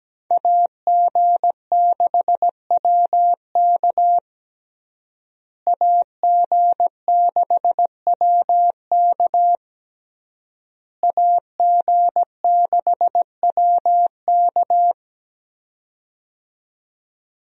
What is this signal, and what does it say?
Morse code, heard as a keyed tone, about 17 wpm, 700 Hz. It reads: AG6WK AG6WK AG6WK
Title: USA Call Signs - 5 Characters - Repeated 3x - 17wpm